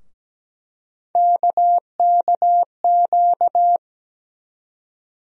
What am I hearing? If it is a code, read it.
Morse code, heard as a keyed tone, about 17 wpm, 700 Hz. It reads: KKQ
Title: USA Call Sign Postfixes - 3 Letters 17wpm